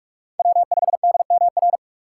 Morse code, heard as a keyed tone, about 44 wpm, 700 Hz. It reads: W5DMR
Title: USA Call Signs - 5 Characters - 3x Character Spacing - 45wpm